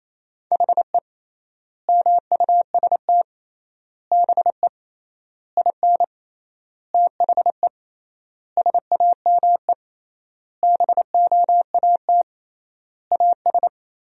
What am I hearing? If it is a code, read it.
Morse code, heard as a keyed tone, about 28 wpm, 700 Hz. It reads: HE MUST BE IN THE SAME BOAT AS
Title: 3x Word-Spacing - Sentences from Top 500 Words - 28wpm